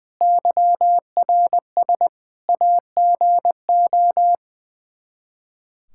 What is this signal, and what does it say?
Morse code, heard as a keyed tone, about 21 wpm, 700 Hz. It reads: YRS AGO
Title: Practice QSO - Exchange 3 - More Info - 20wpm